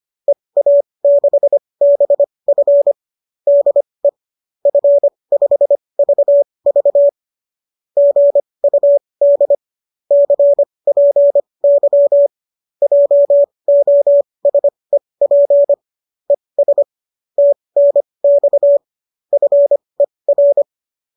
Morse code, heard as a keyed tone, about 25 wpm, spaced as for 20 wpm, 570 Hz. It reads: EA6BF DE F5VV GUD CPY JOSEP ES TNX FER
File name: qso214 20